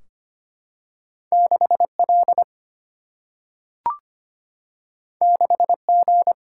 Morse code, heard as a keyed tone, about 25 wpm, 700 Hz. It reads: 6L 6G
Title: Single-Letter Single-Number 25wpm